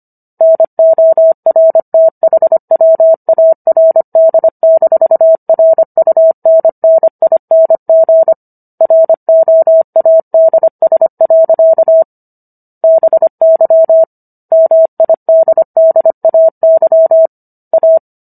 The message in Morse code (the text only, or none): NORTHWARD-RUNNING ROADS. BY MIDDAY A